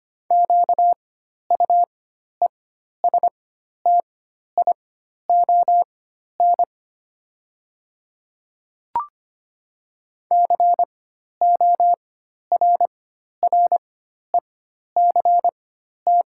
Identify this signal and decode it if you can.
Morse code, heard as a keyed tone, about 25 wpm, spaced as for 12 wpm, 700 Hz. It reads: QUESTION CORRECT
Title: 4x Character Spacing - Top 500 Words - 25wpm